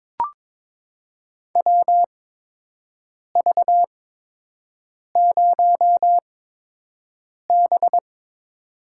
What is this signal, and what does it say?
Morse code, heard as a keyed tone, about 22 wpm, spaced as for 6 wpm, 700 Hz. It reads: WV0B
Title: USA Call Signs - 4 Characters - 8x Character Spacing - 22wpm